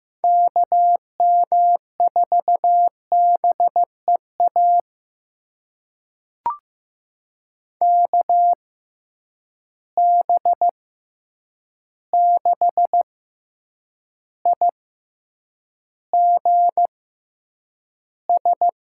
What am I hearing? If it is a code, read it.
Morse code, heard as a keyed tone, about 15 wpm, spaced as for 8 wpm, 700 Hz. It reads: KM4BEA KB6IGS
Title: USA Call Signs - 6 Characters - 6x Character Spacing - 15wpm